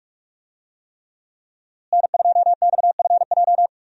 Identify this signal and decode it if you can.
Morse code, heard as a keyed tone, about 45 wpm, 700 Hz. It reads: N2XFJ
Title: USA Call Signs - 5 Characters - Repeated 3x - 45wpm